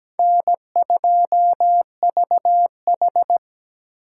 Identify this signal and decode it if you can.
Morse code, heard as a keyed tone, about 17 wpm, 700 Hz. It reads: N2VH
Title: USA Call Signs - 2x Character Spacing - 17wpm